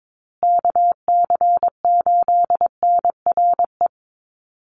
Morse code, heard as a keyed tone, about 22 wpm, 700 Hz. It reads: KC8NRE
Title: USA Call Signs Repeated 3x 22wpm